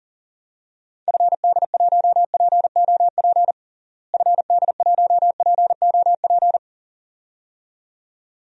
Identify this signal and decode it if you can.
Morse code, heard as a keyed tone, about 37 wpm, 700 Hz. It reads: FD1POP FD1POP
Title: DXCC-100 Call Signs - 6 Characters - Repeated 2x - 40wpm